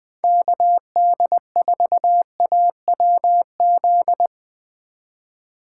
Morse code, heard as a keyed tone, about 20 wpm, 700 Hz. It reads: KD4AWZ